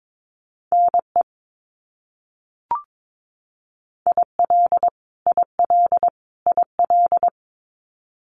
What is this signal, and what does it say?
Morse code, heard as a keyed tone, about 22 wpm, 700 Hz. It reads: NE IL IL IL
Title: US States Repeated 3x 22wpm